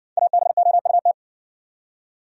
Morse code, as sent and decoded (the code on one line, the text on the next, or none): .- -... --- ..- -
ABOUT